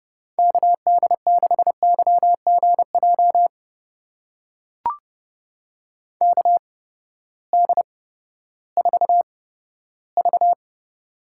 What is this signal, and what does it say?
Morse code, heard as a keyed tone, about 31 wpm, spaced as for 15 wpm, 700 Hz. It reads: KD6YGJ KD4V
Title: USA Call Signs - 8x Character Spacing - 30wpm